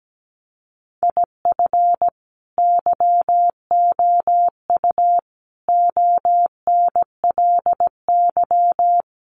Morse code, heard as a keyed tone, about 17 wpm, 700 Hz. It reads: IF YOU ONLY
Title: Sentences from Top 400 Words 17wpm